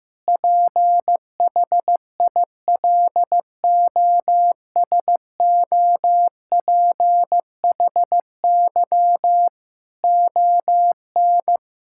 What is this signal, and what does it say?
Morse code, heard as a keyed tone, about 16 wpm, 700 Hz. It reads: PHILOSOPHY ON
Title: Sentences from Top 1000 Words - Review Part 6 - 15wpm